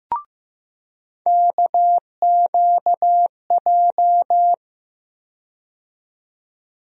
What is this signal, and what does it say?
Morse code, heard as a keyed tone, about 15 wpm, 700 Hz. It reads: KQJ
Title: USA Call Sign Postfixes - 3 Letters 15wpm